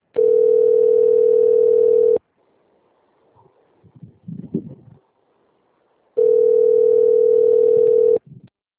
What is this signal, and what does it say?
Ringback tone (North America), 440 + 480 Hz.